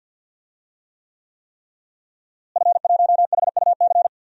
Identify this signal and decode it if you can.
Morse code, heard as a keyed tone, about 52 wpm, 700 Hz. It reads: F1HUC
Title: DXCC-100 Call Signs - 5 Characters - Repeated 3x - 50wpm